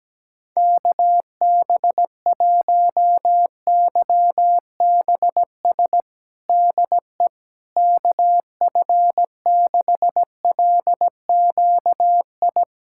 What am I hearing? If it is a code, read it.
Morse code, heard as a keyed tone, about 17 wpm, 700 Hz. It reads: KB1YBS DE KF6LQI
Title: USA Callsign DE Callsign - 6 Characters - 17wpm